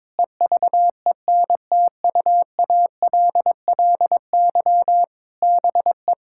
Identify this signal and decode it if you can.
Morse code, heard as a keyed tone, about 22 wpm, 700 Hz. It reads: EVENTUALLY BE
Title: Sentences from Top 1000 Words - Review Part 6 - 22wpm